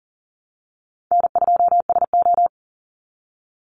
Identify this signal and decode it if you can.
Morse code, heard as a keyed tone, about 40 wpm, 700 Hz. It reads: N2SO
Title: Easy Call Signs Repeated 3x - 40wpm